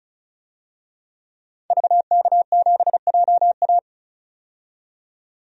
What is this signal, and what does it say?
Morse code, heard as a keyed tone, about 34 wpm, 700 Hz. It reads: VK7JA
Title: DXCC-100 Call Signs - 5 Characters - 35wpm